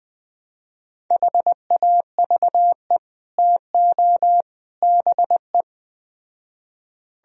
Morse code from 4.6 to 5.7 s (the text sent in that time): BE